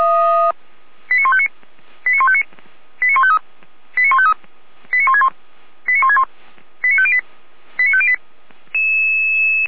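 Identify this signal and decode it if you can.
Five-tone selective call (ZVEI), heard as quick runs of five tones, tones of about 70 ms. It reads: 89159 89159 89153 89153 89161 89161 89508 89508
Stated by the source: ZVEI five-tone selective call received by SDR, from server of feuerwehr-orlamuende (fire brigade Orlamünde).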